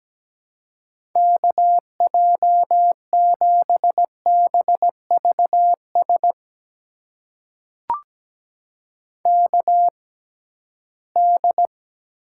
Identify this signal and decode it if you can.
Morse code, heard as a keyed tone, about 17 wpm, spaced as for 13 wpm, 700 Hz. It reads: KJ7BVS KD
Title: USA Call Signs - 6 Characters - 6x Character Spacing - 17wpm